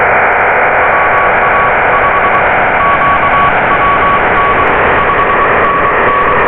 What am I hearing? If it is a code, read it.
Morse code, heard as a keyed tone, about 23 wpm, spaced as for 20 wpm, 1130 Hz, with noise. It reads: WA5QWN 5NN
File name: Radio / SDR Audio / audio z 13-42-58 12-09-2022